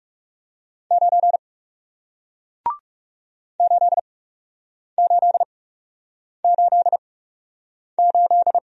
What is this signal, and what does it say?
Morse code, heard as a keyed tone, about 40 wpm, spaced as for 8 wpm, 700 Hz. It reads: 9 8888